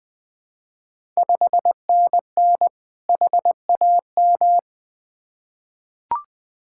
Morse code, heard as a keyed tone, about 20 wpm, 690 Hz. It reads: 5NN HAM
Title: Ontario QSO Party Exchange 20wpm